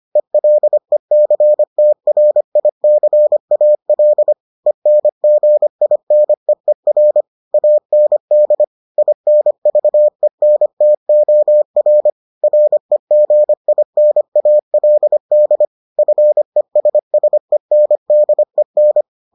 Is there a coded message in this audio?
Yes: Morse code from 0.2 to 19.0 s, heard as a keyed tone, about 25 wpm, 600 Hz. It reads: ELECTRICAL ENGINEER AND INVENTOR REGINALD FESSENDEN